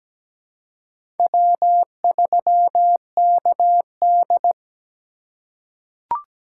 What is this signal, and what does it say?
Morse code, heard as a keyed tone, about 17 wpm, 700 Hz. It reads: W3KD
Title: USA Call Signs - 4 Characters - 3x Character Spacing - 17wpm